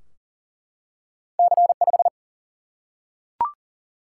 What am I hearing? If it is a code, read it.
Morse code, heard as a keyed tone, about 39 wpm, 700 Hz. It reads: C5
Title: Single-Letter Single-Number 40wpm